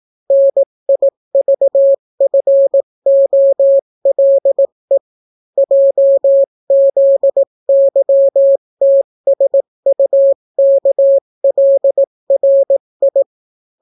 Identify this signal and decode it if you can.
Morse code, heard as a keyed tone, about 18 wpm, 550 Hz. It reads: NIVFOLE JZYTSUKLRI